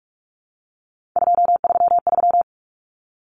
Morse code, heard as a keyed tone, about 44 wpm, 700 Hz. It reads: 233
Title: Numbers - 3-Digit 45wpm